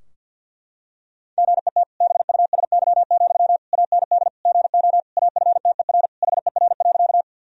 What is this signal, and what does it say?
Morse code, heard as a keyed tone, about 50 wpm, 700 Hz. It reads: GET BUSY, AND GO AFTER HER.
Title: Sentences from Top 1000 Words - Review Part 3 - 50wpm